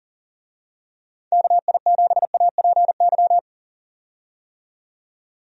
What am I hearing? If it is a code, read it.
Morse code, heard as a keyed tone, about 37 wpm, 700 Hz. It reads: KI7APY
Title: USA Call Signs - 6 Characters - 40wpm